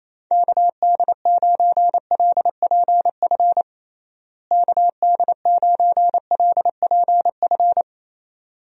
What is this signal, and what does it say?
Morse code, heard as a keyed tone, about 27 wpm, 700 Hz. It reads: KD9LPF KD9LPF
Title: USA Call Signs - 6 Characters - Repeated 2x - 28wpm